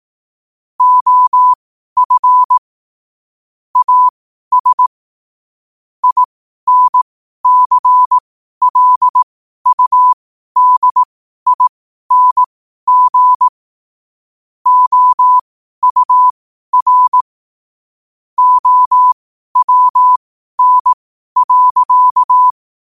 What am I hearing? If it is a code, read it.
Morse code, heard as a keyed tone, about 18 wpm, spaced as for 12 wpm, 1000 Hz. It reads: OF AS INCLUDING OUR OWN.
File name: qotdfiles / 12wpm